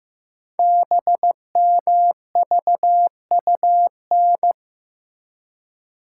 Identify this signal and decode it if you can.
Morse code, heard as a keyed tone, about 15 wpm, 700 Hz. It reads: BMVUN
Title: Code Groups - 5 Letters - 15wpm